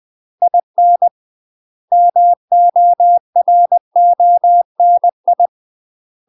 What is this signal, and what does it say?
Morse code, heard as a keyed tone, about 20 wpm, 700 Hz. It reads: IN MORONI
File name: UK MASTER QSOs TRAINING / 20WPM / UK MASTER QSOs 0035